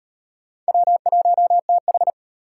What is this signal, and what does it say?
Morse code, heard as a keyed tone, about 30 wpm, 700 Hz. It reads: W1TH
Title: Speed Racing - USA 1x2 Call Signs 25wpm